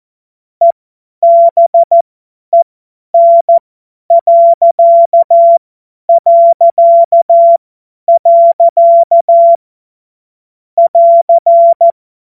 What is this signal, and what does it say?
Morse code, heard as a keyed tone, about 14 wpm, spaced as for 10 wpm, 680 Hz. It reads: EBEN... <AR>
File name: Morsekurs / Witze / Witz 21.November 2023 50 F70